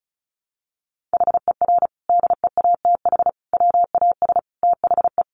Morse code, heard as a keyed tone, about 34 wpm, 700 Hz. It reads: HER DEATH WAS THE
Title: Sentences from Top 1300 Words 35wpm